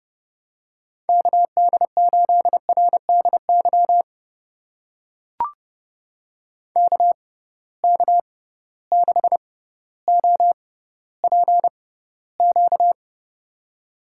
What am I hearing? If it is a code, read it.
Morse code, heard as a keyed tone, about 32 wpm, spaced as for 16 wpm, 700 Hz. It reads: KD8RDY KK6OPQ
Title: USA Call Signs - 6 Characters - 6x Character Spacing - 30wpm